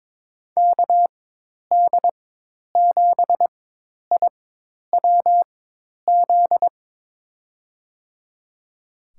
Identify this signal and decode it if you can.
Morse code, heard as a keyed tone, about 21 wpm, spaced as for 10 wpm, 700 Hz. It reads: KD7IWZ
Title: USA Call Signs - 6 Characters - 4x Character Spacing - 22wpm